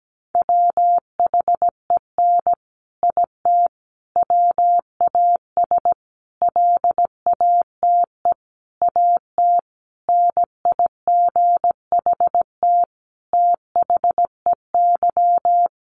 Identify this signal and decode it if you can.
Morse code, heard as a keyed tone, about 17 wpm, 700 Hz. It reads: WHEN IT WAS LATE AT NIGHT THEY
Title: Sentences from Top 600 Words 17wpm